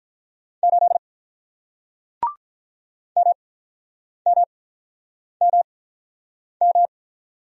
Morse code, heard as a keyed tone, about 48 wpm, spaced as for 9 wpm, 700 Hz. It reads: 8 MMMM